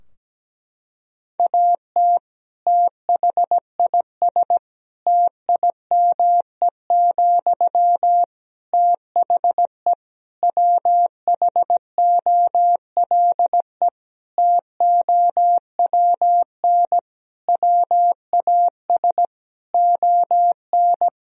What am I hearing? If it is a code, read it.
Morse code, heard as a keyed tone, about 17 wpm, 700 Hz. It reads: AT THIS TIME, THE WHOLE TOWN WAS ON
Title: Sentences from Top 1000 Words - Review Part 2 - 17wpm